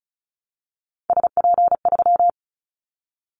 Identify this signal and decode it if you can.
Morse code, heard as a keyed tone, about 34 wpm, 700 Hz. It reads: SP3
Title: Callsign Prefixes for Top 25 DXCC - Repeated 3x - 35wpm